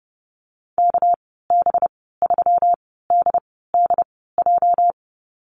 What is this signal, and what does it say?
Morse code, heard as a keyed tone, about 30 wpm, spaced as for 17 wpm, 700 Hz. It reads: KB3DDJ
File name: Lesson 37g - Letter x - Call Signs 30wpm